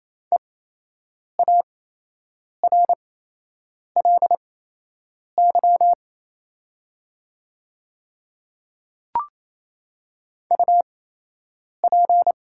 Morse code, heard as a keyed tone, about 28 wpm, spaced as for 8 wpm, 700 Hz. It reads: EARLY UP
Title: 8x Character Spacing - Top 500 Words - 28wpm